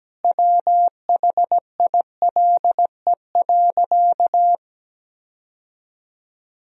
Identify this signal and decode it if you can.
Morse code, heard as a keyed tone, about 17 wpm, 700 Hz. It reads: WHILE.
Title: Sentences from Top 1000 Words - Review Part 1 - 17wpm